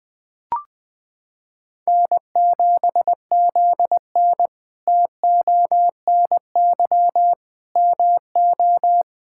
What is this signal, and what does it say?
Morse code, heard as a keyed tone, about 20 wpm, 700 Hz. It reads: N7ZN TONY MO
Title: K1USN SST QSO Practice - 20wpm